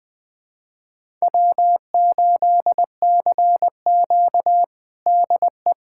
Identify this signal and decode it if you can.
Morse code, heard as a keyed tone, about 20 wpm, 700 Hz. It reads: W8CQ DE
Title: USA Callsign DE Callsign - 4 Characters - 20wpm